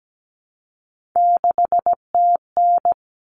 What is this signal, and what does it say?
Morse code, heard as a keyed tone, about 17 wpm, 700 Hz. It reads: 6TN
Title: Common Cut Numbers - 3-Digit 17wpm